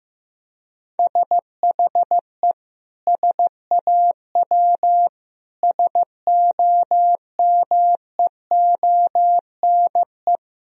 Morse code, heard as a keyed tone, about 16 wpm, 700 Hz. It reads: SHE SAW SOMEONE
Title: Sentences from Top 1000 Words - Review Part 4 - 15wpm